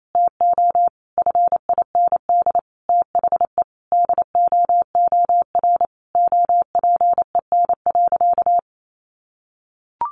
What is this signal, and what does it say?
Morse code, heard as a keyed tone, about 28 wpm, 700 Hz. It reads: TO FIND THE DOOR OPEN.
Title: Sentences from Top 1000 Words 28wpm